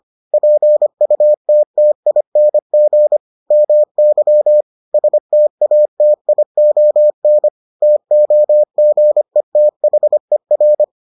Morse code, heard as a keyed tone, about 25 wpm, 600 Hz. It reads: PUTTING MY STATION TOGETHER